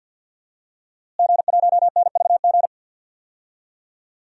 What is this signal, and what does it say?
Morse code, heard as a keyed tone, about 46 wpm, 700 Hz. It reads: G1NVG
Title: DXCC-100 Call Signs - 5 Characters - Repeated 2x - 50wpm